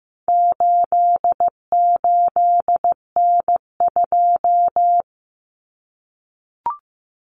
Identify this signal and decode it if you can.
Morse code, heard as a keyed tone, about 15 wpm, 700 Hz. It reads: 88N2
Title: Common Cut Numbers - 4-Digit 15wpm